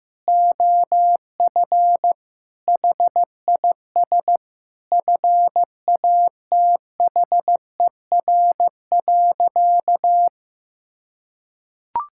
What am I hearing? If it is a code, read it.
Morse code, heard as a keyed tone, about 15 wpm, 700 Hz. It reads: OF HIS FATHER.
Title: Sentences from Top 1000 Words - Review Part 6 - 15wpm